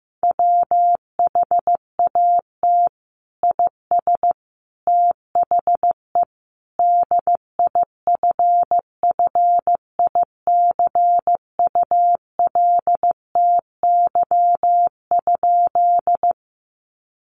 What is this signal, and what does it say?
Morse code, heard as a keyed tone, about 15 wpm, 700 Hz. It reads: WHAT IS THE DIFFICULTY?
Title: Sentences from Top 1500 Words 15wpm